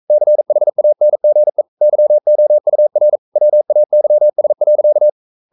Morse code, heard as a keyed tone, about 42 wpm, 600 Hz. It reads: CHANGE YOUR WAYS.
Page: Romanen Never af Ken Follet på engelsk.